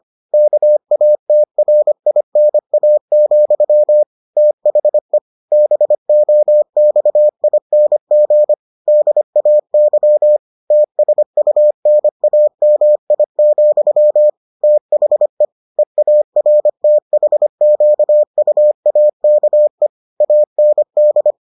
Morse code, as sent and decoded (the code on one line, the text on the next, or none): -.- .- - .-. .. -. .- --..-- / - .... / -... --- -..- .. -. --. / -.. .- -.-- / - ... ..- -. .- -- .. --..-- / - .... / . .- .-. - .... --.- ..- .- -.- . / .- -. -..
KATRINA, THE BOXING DAY TSUNAMI, THE EARTHQUAKE AND